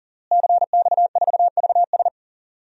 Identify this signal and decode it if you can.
Morse code, heard as a keyed tone, about 40 wpm, 700 Hz. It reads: CX4VS
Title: DXCC-100 Call Signs - 5 Characters - Repeated 3x - 40wpm